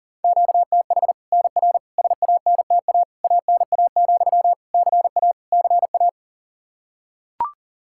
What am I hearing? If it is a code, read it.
Morse code, heard as a keyed tone, about 40 wpm, 700 Hz. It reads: QTH NR SANTA ANA, CA CA
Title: Practice QSO - Exchange 1 - Beginning - 40wpm